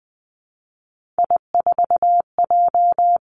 Morse code, heard as a keyed tone, about 20 wpm, 700 Hz. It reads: I4J
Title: Three Challenging Letter-Number Combinations 20wpm